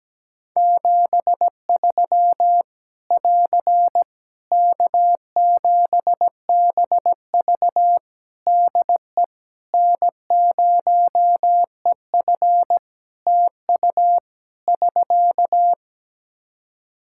Morse code, heard as a keyed tone, about 17 wpm, 700 Hz. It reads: 73 <AR> K7BV DE N0EF TU <SK>
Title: Practice QSO - Exchange 4 - Ending - 17wpm